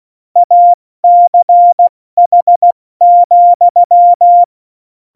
Morse code, heard as a keyed tone, about 16 wpm, 700 Hz. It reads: ACH,
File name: Morsekurs / Witze / Witz 07. Januar 2020, 70 F80